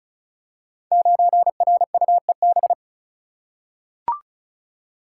Morse code, heard as a keyed tone, about 36 wpm, 700 Hz. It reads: 9RUEB
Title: Code Groups - 5 Letter:Numbers - 35wpm